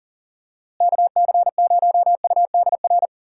Morse code, heard as a keyed tone, about 40 wpm, 700 Hz. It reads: KC0UDR
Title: USA Call Signs - 6 Characters - Repeated 2x - 40wpm